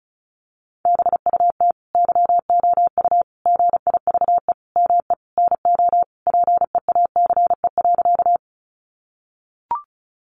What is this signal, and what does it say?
Morse code, heard as a keyed tone, about 34 wpm, 700 Hz. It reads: BUT YOU GIVE ME NO PEACE.
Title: Sentences from Top 1500 Words 35wpm